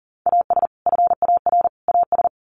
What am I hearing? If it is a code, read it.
Morse code, heard as a keyed tone, about 40 wpm, 700 Hz. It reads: AS FAR AS
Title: Sets of 3 Words 40wpm